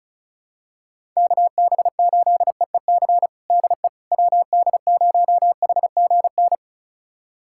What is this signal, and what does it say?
Morse code, heard as a keyed tone, about 36 wpm, 700 Hz. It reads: KB8EEC DE WD0HGN
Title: USA Callsign DE Callsign - 6 Characters - 35wpm